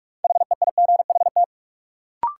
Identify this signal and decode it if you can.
Morse code, heard as a keyed tone, about 44 wpm, 700 Hz. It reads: HEIGHT
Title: Sentences from Top 1600 Words - New Word Drill - 45wpm